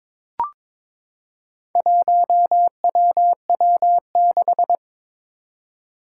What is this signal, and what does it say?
Morse code, heard as a keyed tone, about 22 wpm, 700 Hz. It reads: 1WW6
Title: Four Challenging Letter-Number Combinations 22wpm